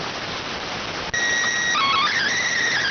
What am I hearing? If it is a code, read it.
SSTV (slow-scan television), PD 290.